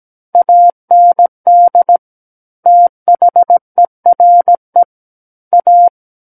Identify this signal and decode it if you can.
Morse code, heard as a keyed tone, about 17 wpm, 700 Hz. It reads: AND THERE A
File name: Book One / 08-Chapter 8 - 17 WPM